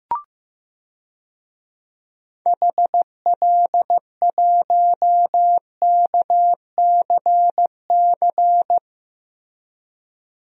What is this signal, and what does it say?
Morse code, heard as a keyed tone, about 15 wpm, 700 Hz. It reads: HL1KCC
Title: DXCC-100 Call Signs - 6 Characters - Repeated 3x - 15wpm